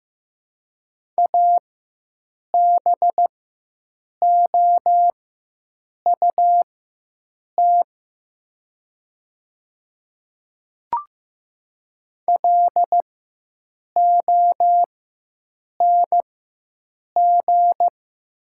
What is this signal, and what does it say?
Morse code, heard as a keyed tone, about 15 wpm, spaced as for 7 wpm, 700 Hz. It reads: ABOUT LONG
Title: 4x Character Spacing - Top 100 Words - 15wpm